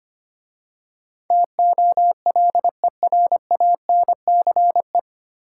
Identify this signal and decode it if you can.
Morse code, heard as a keyed tone, about 25 wpm, 700 Hz. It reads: TOLERANCE
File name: practice / 25wpm / 9 Letter Words - First 3 Letters Given - 25wpm